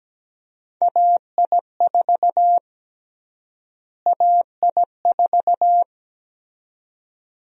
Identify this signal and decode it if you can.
Morse code, heard as a keyed tone, about 17 wpm, 700 Hz. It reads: AI4 AI4
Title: USA Call Sign Prefixes - Repeated 2x - 17wpm